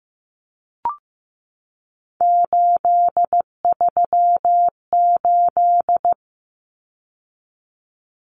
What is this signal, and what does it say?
Morse code, heard as a keyed tone, about 15 wpm, 700 Hz. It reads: 838
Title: Numbers - 3-Digit 15wpm